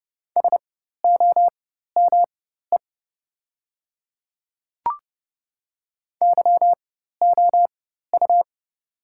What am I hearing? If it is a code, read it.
Morse code, heard as a keyed tone, about 30 wpm, spaced as for 14 wpm, 700 Hz. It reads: SOME YOU
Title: 4x Character Spacing - Top 200 Words - 30wpm